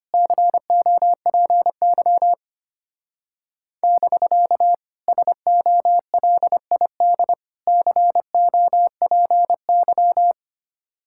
Morse code, heard as a keyed tone, about 30 wpm, 700 Hz. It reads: COPY <BK> SOLID COPY
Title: Speed-Racing Common QSO Phrases - 25wpm